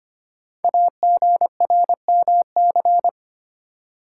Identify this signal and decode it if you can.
Morse code, heard as a keyed tone, about 25 wpm, 700 Hz. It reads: AGRMC
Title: Code Groups - 5 Letters - 25wpm